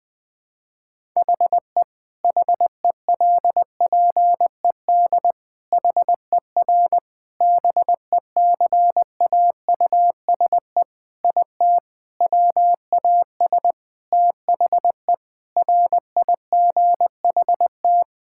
Morse code, heard as a keyed tone, about 20 wpm, 700 Hz. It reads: HE HELPED HER BECAUSE IT WAS THE RIGHT
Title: Sentences with 8-Word Phrases - 20wpm